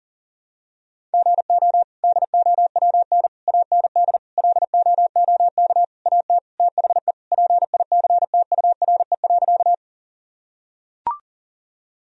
Morse code, heard as a keyed tone, about 40 wpm, 700 Hz. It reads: GO DOWN AND LOOK AT THE PICTURE.
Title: Sentences from Top 1000 Words - Review Part 1 - 40wpm